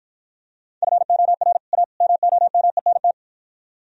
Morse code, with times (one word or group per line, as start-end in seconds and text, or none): 0.8-1.6 FOR
1.7-1.8 A
2.0-3.1 MOMENT